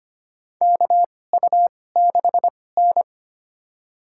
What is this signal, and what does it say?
Morse code, heard as a keyed tone, about 25 wpm, spaced as for 18 wpm, 700 Hz. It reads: KU6N
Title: USA Call Signs - 4 Characters - 2x Character Spacing - 25wpm